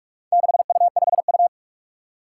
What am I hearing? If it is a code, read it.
Morse code, heard as a keyed tone, about 44 wpm, 700 Hz. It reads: 6U5U
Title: Four Challenging Letter-Number Combinations 45wpm